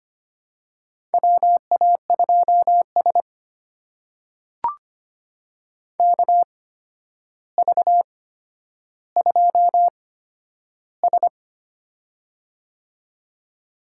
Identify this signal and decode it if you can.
Morse code, heard as a keyed tone, about 25 wpm, spaced as for 11 wpm, 700 Hz. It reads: WA2S KV2S
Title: USA Call Signs - 4 Characters - 8x Character Spacing - 25wpm